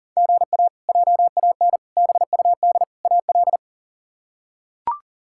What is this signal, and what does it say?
Morse code, heard as a keyed tone, about 40 wpm, 700 Hz. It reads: GA JAN BUD AL
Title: K1USN SST QSO Practice - 40wpm